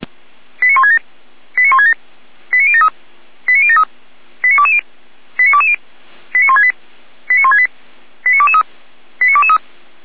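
Five-tone selective call (ZVEI), heard as quick runs of five tones, tones of about 70 ms. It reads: 89167 89167 89073 89073 89220 89220 89168 89168 89223 89223
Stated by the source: ZVEI five-tone selective call 89060 received by SDR.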